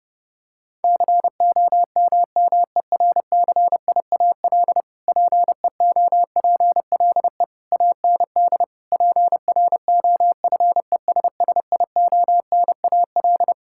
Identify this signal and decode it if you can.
Morse code, heard as a keyed tone, about 30 wpm, 700 Hz. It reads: COMMERCIAL PEOPLE AND PROFESSIONAL
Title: Sentences from Top 1000 Words - Review Part 5 - 30wpm